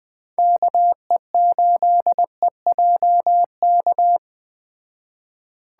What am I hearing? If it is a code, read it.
Morse code, heard as a keyed tone, about 20 wpm, 700 Hz. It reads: KE8EJK
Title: Speed Racing - USA 2x3 Call Signs 17wpm